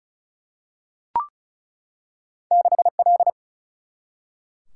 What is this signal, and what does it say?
Morse code, heard as a keyed tone, about 36 wpm, 700 Hz. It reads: 6L
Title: Single-Letter Single-Number 35wpm